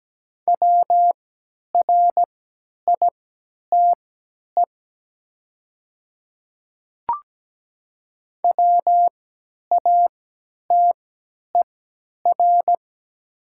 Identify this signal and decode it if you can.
Morse code, heard as a keyed tone, about 17 wpm, spaced as for 10 wpm, 700 Hz. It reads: WRITE WATER
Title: 3x Character Spacing - Top 200 Words - 17wpm